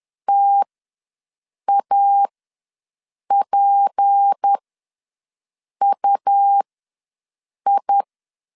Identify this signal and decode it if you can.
Morse code, heard as a keyed tone, about 10 wpm, spaced as for 6 wpm, 800 Hz. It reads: TAPUI
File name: LEZIONI / 8 YJ45 / 8 YJ ws 10wpm 4043